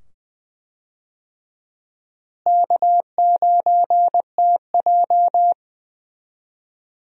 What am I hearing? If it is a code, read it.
Morse code, heard as a keyed tone, about 20 wpm, 700 Hz. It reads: K9TJ
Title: USA Call Signs - 4 Characters - Repeated 2x - 20wpm